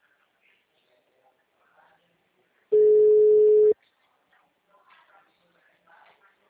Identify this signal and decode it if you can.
Ringback tone (Europe, 425 Hz).